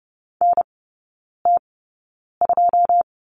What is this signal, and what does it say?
Morse code, heard as a keyed tone, about 30 wpm, spaced as for 9 wpm, 700 Hz. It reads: NT2
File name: Lesson 19e - Letter w - Three Characters 30wpm